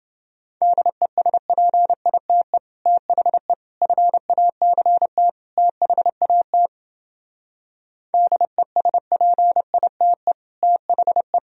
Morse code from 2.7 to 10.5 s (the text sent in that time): THE FACT THAT DESPITE